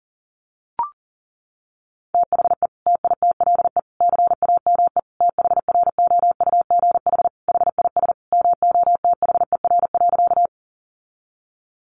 Morse code, heard as a keyed tone, about 40 wpm, 700 Hz. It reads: THE TITLE CAME THROUGH HIS MOTHER.
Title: Sentences from Top 1200 Words 40wpm